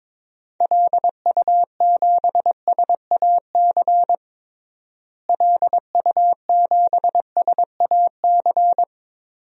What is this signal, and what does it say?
Morse code, heard as a keyed tone, about 22 wpm, 700 Hz. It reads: LU7SAC LU7SAC
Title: DXCC-100 Call Signs - 6 Characters - Repeated 3x - 22wpm